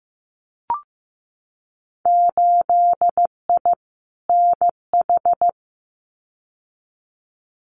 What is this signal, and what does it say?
Morse code, heard as a keyed tone, about 15 wpm, 700 Hz. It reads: 8I NH
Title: Winter Field Day Exchanges Encore 15wpm